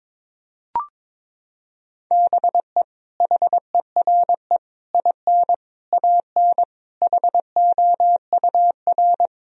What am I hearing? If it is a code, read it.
Morse code, heard as a keyed tone, about 22 wpm, 700 Hz. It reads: BE HERE IN AN HOUR
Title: Sentences from Top 1000 Words - Review Part 4 - 22wpm